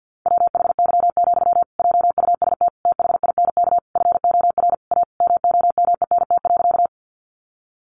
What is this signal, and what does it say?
Morse code, heard as a keyed tone, about 48 wpm, 700 Hz. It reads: WHY, JUST THINK FOR A MOMENT.
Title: Sentences from Top 1000 Words 50wpm